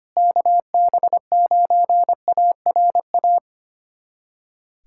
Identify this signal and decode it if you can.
Morse code, heard as a keyed tone, about 25 wpm, 700 Hz. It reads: KB9ARA
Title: USA Callsign DE Callsign - 6 Characters - 25wpm